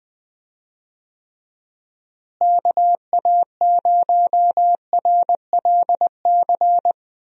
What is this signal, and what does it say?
Morse code, heard as a keyed tone, about 20 wpm, 700 Hz. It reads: KA0RLC
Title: USA Call Signs - 6 Characters - Repeated 3x - 20wpm